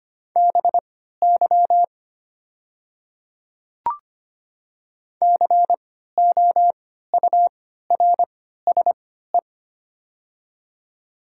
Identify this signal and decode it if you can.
Morse code, heard as a keyed tone, about 25 wpm, spaced as for 14 wpm, 700 Hz. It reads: BY COURSE
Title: 3x Character Spacing - Top 500 Words - 25wpm